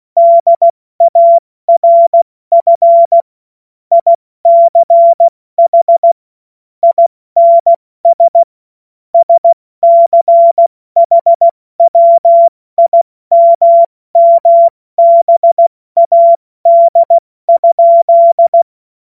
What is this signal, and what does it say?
Morse code, heard as a keyed tone, about 16 wpm, 680 Hz. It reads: DARF ICH INS SCHWIMMBAD?
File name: Morsekurs / Witze / Witz 21. September 2021, 70 F80